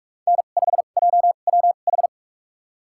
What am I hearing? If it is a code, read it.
Morse code, heard as a keyed tone, about 46 wpm, spaced as for 33 wpm, 700 Hz. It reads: N5JWH
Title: USA Call Signs - 5 Characters - 2x Character Spacing - 45wpm